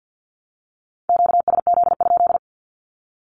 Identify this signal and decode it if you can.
Morse code, heard as a keyed tone, about 50 wpm, 700 Hz. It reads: QSZ?
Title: Q-Codes 50wpm